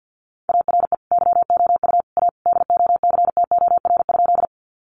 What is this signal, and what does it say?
Morse code, heard as a keyed tone, about 50 wpm, 700 Hz. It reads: ARE YOU A DOCTOR?
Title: Sentences from Top 1500 Words 50wpm